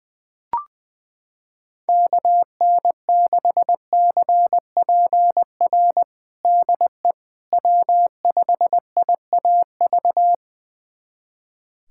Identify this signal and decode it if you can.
Morse code, heard as a keyed tone, about 21 wpm, 700 Hz. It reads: KN6CPR DE W5IAV
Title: USA Callsign DE Callsign 20wpm